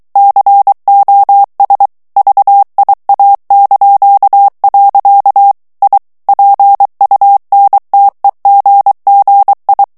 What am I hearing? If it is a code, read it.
Morse code, heard as a keyed tone, about 23 wpm, 800 Hz. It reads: COS VIA). I PUNTEGGI
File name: ITALIEN texte clair 2 vitesse 22 mots minute